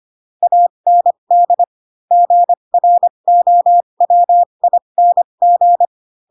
Morse code, heard as a keyed tone, about 25 wpm, spaced as for 22 wpm, 700 Hz. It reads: AND GROWING
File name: Pinocchio / Pinocchio ch2 18 25 ik1hge